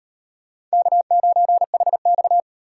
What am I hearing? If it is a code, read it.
Morse code, heard as a keyed tone, about 38 wpm, 700 Hz. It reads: K9HX